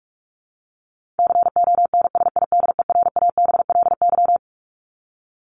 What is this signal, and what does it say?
Morse code, heard as a keyed tone, about 44 wpm, 700 Hz. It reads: CONSIDERABLY
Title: LY Suffix - Top 100 Words with Base Words 45wpm